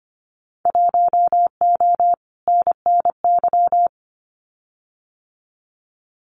Morse code, heard as a keyed tone, about 25 wpm, 700 Hz. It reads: 1O NNY
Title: Winter Field Day Exchanges 25wpm